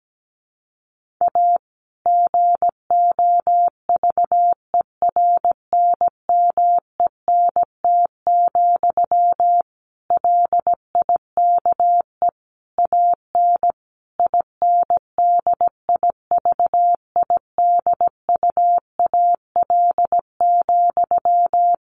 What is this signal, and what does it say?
Morse code, heard as a keyed tone, about 17 wpm, 700 Hz. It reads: A GOVERNMENT, LIKE AN INDIVIDUAL,
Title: Sentences from Top 1400 Words 17wpm